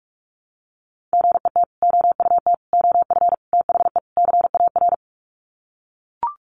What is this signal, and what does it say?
Morse code, heard as a keyed tone, about 44 wpm, 700 Hz. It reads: GET OUT OF THE CAR
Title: Sets of 5 Words Encore2 45wpm